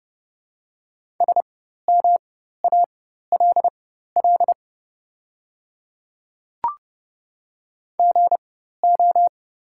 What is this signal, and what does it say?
Morse code, heard as a keyed tone, about 31 wpm, spaced as for 14 wpm, 700 Hz. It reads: SMALL GO
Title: 4x Character Spacing - Top 500 Words - 30wpm